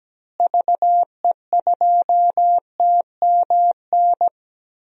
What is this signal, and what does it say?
Morse code, heard as a keyed tone, about 17 wpm, 700 Hz. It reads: VE2TMN
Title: DXCC-100 Call Signs - 6 Characters - Repeated 2x - 17wpm